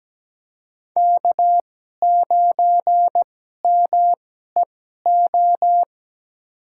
Morse code, heard as a keyed tone, about 17 wpm, spaced as for 12 wpm, 700 Hz. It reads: K9MEO
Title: USA Call Signs - 5 Characters - 2x Character Spacing - 17wpm